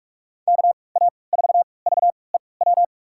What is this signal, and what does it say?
Morse code, heard as a keyed tone, about 42 wpm, spaced as for 25 wpm, 700 Hz. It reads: KA4VEW
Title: USA Call Signs - 6 Characters - 3x Character Spacing - 45wpm